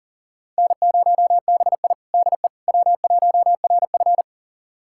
Morse code, heard as a keyed tone, about 40 wpm, 700 Hz. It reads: N0BI DE W1RF
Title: USA Callsign DE Callsign - 4 Characters - 40wpm